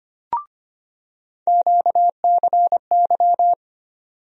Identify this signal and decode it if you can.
Morse code, heard as a keyed tone, about 25 wpm, 700 Hz. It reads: QCY
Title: USA Call Sign Postfixes - 3 Letters 25wpm